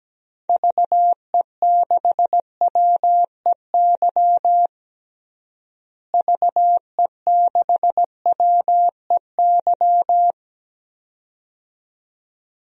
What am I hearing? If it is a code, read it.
Morse code, heard as a keyed tone, about 17 wpm, 700 Hz. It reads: VE6WEY VE6WEY
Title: DXCC-100 Call Signs - 6 Characters - Repeated 2x - 17wpm